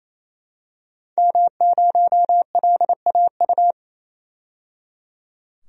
Morse code, heard as a keyed tone, about 28 wpm, 700 Hz. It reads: M0LAU
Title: DXCC-100 Call Signs - 5 Characters - 28wpm